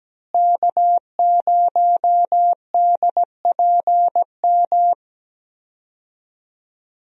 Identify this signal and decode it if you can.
Morse code, heard as a keyed tone, about 17 wpm, 700 Hz. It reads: K0DPM
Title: USA Call Signs - 5 Characters - Repeated 2x - 17wpm